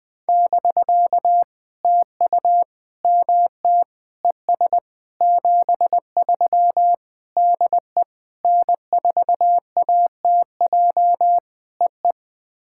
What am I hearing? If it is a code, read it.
Morse code, heard as a keyed tone, about 20 wpm, 700 Hz. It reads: <BK> TU MT ES 73 DE N4ATJ EE
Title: POTA QSO Practice - 20wpm